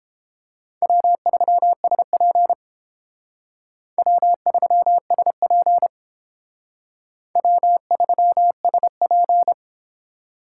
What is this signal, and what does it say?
Morse code, heard as a keyed tone, about 30 wpm, 700 Hz. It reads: W3SP W3SP W3SP